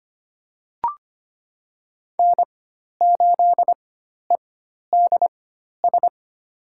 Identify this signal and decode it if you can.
Morse code, heard as a keyed tone, about 25 wpm, spaced as for 12 wpm, 700 Hz. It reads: N8EDS
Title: USA Call Signs - 5 Characters - 4x Character Spacing - 25wpm